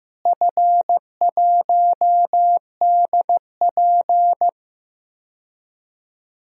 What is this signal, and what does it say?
Morse code, heard as a keyed tone, about 15 wpm, 700 Hz. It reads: F1DP
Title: DXCC-100 Call Signs - 4 Characters - Repeated 2x - 15wpm